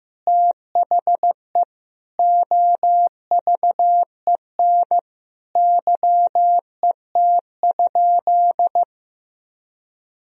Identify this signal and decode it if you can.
Morse code, heard as a keyed tone, about 16 wpm, 700 Hz. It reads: THE OVEN YET?
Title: Sentences from Top 1000 Words - Review Part 4 - 15wpm